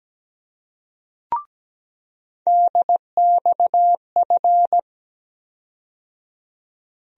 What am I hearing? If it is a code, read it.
Morse code, heard as a keyed tone, about 17 wpm, 700 Hz. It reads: DXF
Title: USA Call Sign Postfixes - 3 Letters 17wpm